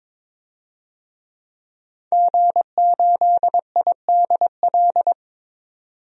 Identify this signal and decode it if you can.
Morse code, heard as a keyed tone, about 23 wpm, 700 Hz. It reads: G8IDL
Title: DXCC-100 Call Signs - 5 Characters - 22wpm